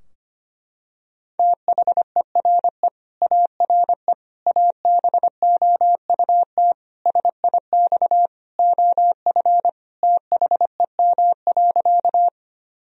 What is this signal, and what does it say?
Morse code, heard as a keyed tone, about 25 wpm, 700 Hz. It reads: THERE ARE ABOUT SIX OF THEM.
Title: Sentences from Top 1000 Words - Review Part 2 - 25wpm